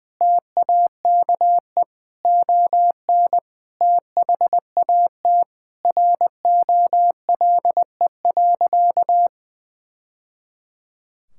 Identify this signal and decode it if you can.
Morse code, heard as a keyed tone, about 21 wpm, 700 Hz. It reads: TAKE ON THAT ROLE.
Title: Sentences from Top 1000 Words - Review Part 6 - 20wpm